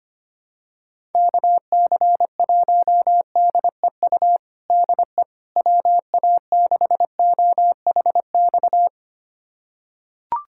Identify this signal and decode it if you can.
Morse code, heard as a keyed tone, about 25 wpm, 700 Hz. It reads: KC1DEU DE WA6OHX
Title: USA Callsign DE Callsign - 6 Characters - 25wpm